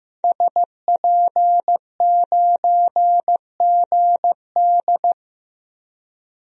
Morse code, heard as a keyed tone, about 15 wpm, 700 Hz. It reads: SP9GD
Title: DXCC-100 Call Signs - 5 Characters - Repeated 3x - 15wpm